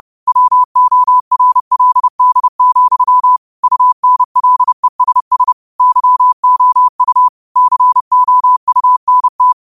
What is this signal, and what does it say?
Morse code, heard as a keyed tone, about 30 wpm, 1000 Hz. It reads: WORLD, UNLESS YOU COUNT